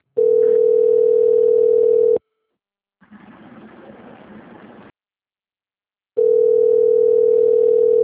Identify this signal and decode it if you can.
Ringback tone (North America), 440 + 480 Hz.
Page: Rio Centro - RJ